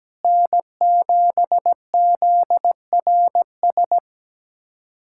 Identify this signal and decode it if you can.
Morse code, heard as a keyed tone, about 17 wpm, 700 Hz. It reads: N7ZRS
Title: USA Callsign DE Callsign - 5 Characters - 17wpm